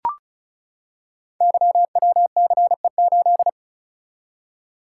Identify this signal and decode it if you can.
Morse code, heard as a keyed tone, about 36 wpm, 690 Hz. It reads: YWCE8
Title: Code Groups - 5 Letter:Numbers - 35wpm